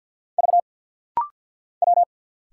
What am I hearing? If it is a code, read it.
Morse code, heard as a keyed tone, about 50 wpm, 690 Hz. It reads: V W
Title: Single Letters - Rapid-Fire 50wpm